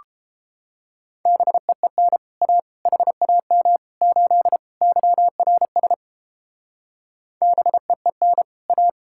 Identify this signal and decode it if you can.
Morse code, heard as a keyed tone, about 31 wpm, 700 Hz. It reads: BEEN A HAM 8 YRS BEEN A